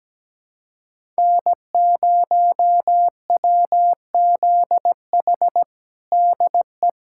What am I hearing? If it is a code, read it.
Morse code, heard as a keyed tone, about 17 wpm, 700 Hz. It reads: N0WZH DE
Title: USA Callsign DE Callsign - 5 Characters - 17wpm